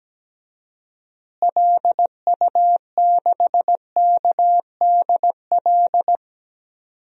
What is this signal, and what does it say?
Morse code, heard as a keyed tone, about 17 wpm, 700 Hz. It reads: LU6KDL
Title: DXCC-100 Call Signs - 6 Characters - Repeated 3x - 17wpm